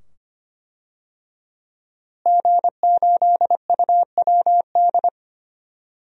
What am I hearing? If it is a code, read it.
Morse code, heard as a keyed tone, about 25 wpm, 700 Hz. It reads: G8UWD
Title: DXCC-100 Call Signs - 5 Characters - Repeated 2x - 25wpm